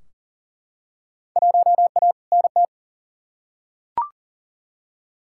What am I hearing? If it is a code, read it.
Morse code, heard as a keyed tone, about 40 wpm, 700 Hz. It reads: 1A NT
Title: Field Day Exchanges Encore - 40wpm